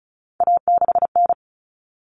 Morse code, heard as a keyed tone, about 34 wpm, 700 Hz. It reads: A6N
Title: All Cut Numbers - 3-Digit 35wpm